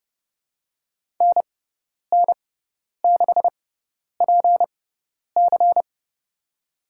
Morse code, heard as a keyed tone, about 30 wpm, spaced as for 10 wpm, 700 Hz. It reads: NN6PC
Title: USA Call Signs - 5 Characters - 6x Character Spacing - 30wpm